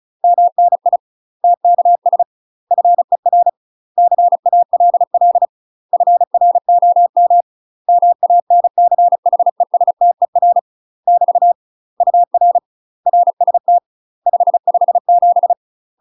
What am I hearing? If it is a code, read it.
Morse code, heard as a keyed tone, about 34 wpm, 700 Hz. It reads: MNI TKS FER CALL FROM MANCHESTER <BT> UR RST 557
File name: UK MASTER QSOs TRAINING / 35WPM / UK MASTER QSOs 0024